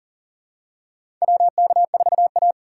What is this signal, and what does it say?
Morse code, heard as a keyed tone, about 40 wpm, 700 Hz. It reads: WK4A
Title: USA Call Signs - 4 Characters - 6x Character Spacing - 40wpm